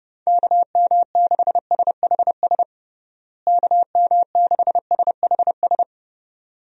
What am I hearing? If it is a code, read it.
Morse code, heard as a keyed tone, about 31 wpm, 700 Hz. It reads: KM6SHS KM6SHS
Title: USA Call Signs - 6 Characters - Repeated 2x - 30wpm